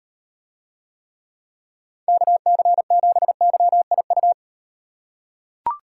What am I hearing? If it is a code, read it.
Morse code, heard as a keyed tone, about 35 wpm, 700 Hz. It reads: KC7YIU